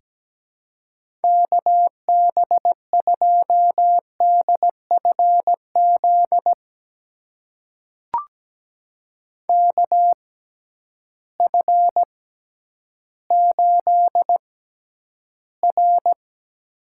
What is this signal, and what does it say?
Morse code, heard as a keyed tone, about 17 wpm, spaced as for 10 wpm, 700 Hz. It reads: KB2DFZ KF8R
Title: USA Call Signs - 6x Character Spacing - 17wpm